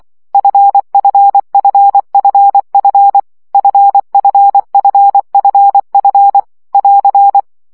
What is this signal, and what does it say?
Morse code, heard as a keyed tone, about 24 wpm, 800 Hz. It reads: FFFFF FFFFF <AR>
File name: CW gyakorlas f 20 baud